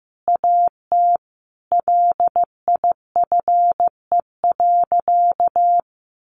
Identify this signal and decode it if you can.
Morse code, heard as a keyed tone, about 15 wpm, 700 Hz. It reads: AT LIFE.
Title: Sentences from Top 1400 Words 15wpm